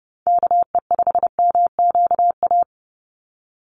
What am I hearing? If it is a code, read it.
Morse code, heard as a keyed tone, about 30 wpm, 700 Hz. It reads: KE5MQA
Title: USA Call Signs 30wpm